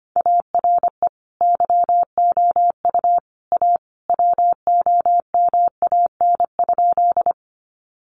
Morse code, heard as a keyed tone, about 25 wpm, 700 Hz. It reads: ARE YOU A WOMAN?
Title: Sentences from Top 1100 Words 25wpm